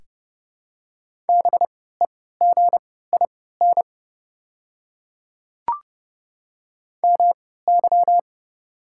Morse code, heard as a keyed tone, about 31 wpm, spaced as for 17 wpm, 700 Hz. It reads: BEGIN MY
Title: 3x Character Spacing - Top 500 Words - 30wpm